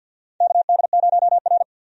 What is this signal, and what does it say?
Morse code, heard as a keyed tone, about 47 wpm, 700 Hz. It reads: KD0R
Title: USA Call Signs - 4 Characters - 50wpm